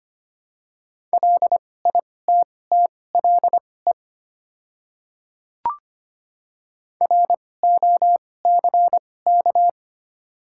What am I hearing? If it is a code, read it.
Morse code, heard as a keyed tone, about 25 wpm, spaced as for 18 wpm, 690 Hz. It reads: LITTLE ROCK
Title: 2x Character Spacing - Top 500 Words - 25wpm